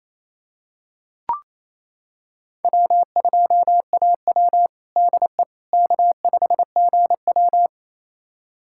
Morse code, heard as a keyed tone, about 28 wpm, 700 Hz. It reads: W2AW DE K5GW
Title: USA Callsign DE Callsign - 4 Characters - 28wpm